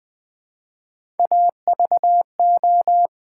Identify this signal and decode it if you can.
Morse code, heard as a keyed tone, about 20 wpm, 700 Hz. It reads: AVO
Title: Build A Word - 1 Letter At A Time - 5-Letter Words - 20wpm